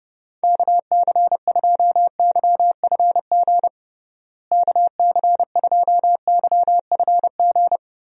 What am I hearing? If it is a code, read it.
Morse code, heard as a keyed tone, about 32 wpm, 700 Hz. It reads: KC2YFG KC2YFG
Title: USA Call Signs - 6 Characters - Repeated 2x - 30wpm